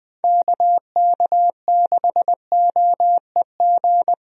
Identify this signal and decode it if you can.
Morse code, heard as a keyed tone, about 21 wpm, 700 Hz. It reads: KK6OEG
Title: USA Call Signs - 6 Characters - Repeated 3x - 20wpm